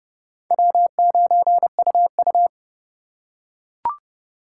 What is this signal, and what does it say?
Morse code, heard as a keyed tone, about 29 wpm, 700 Hz. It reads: W9UU
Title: USA Call Signs - 4 Characters - 2x Character Spacing - 30wpm